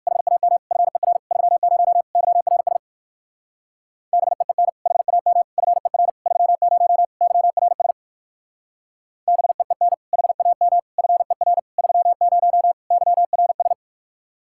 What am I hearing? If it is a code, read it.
Morse code, heard as a keyed tone, about 59 wpm, 700 Hz. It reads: HAM FER 30 YRS BEEN HAM FER 30 YRS BEEN HAM FER 30 YRS